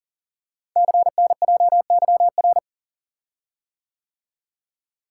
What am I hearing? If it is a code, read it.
Morse code, heard as a keyed tone, about 43 wpm, 700 Hz. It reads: CNJYR
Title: Code Groups - 5 Letters - 40wpm